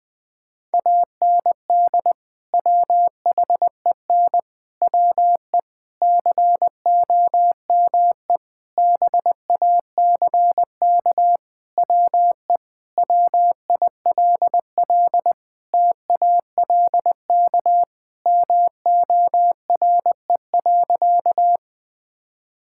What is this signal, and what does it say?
Morse code, heard as a keyed tone, about 20 wpm, 700 Hz. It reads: AND WHEN WE COME BACK WE WILL TALK MORE.
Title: Sentences with 5-Word Phrases - 20wpm